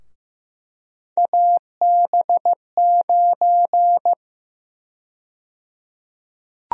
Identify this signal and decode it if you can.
Morse code, heard as a keyed tone, about 15 wpm, 700 Hz. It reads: AB9
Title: USA Call Sign Prefixes - Repeated 2x - 15wpm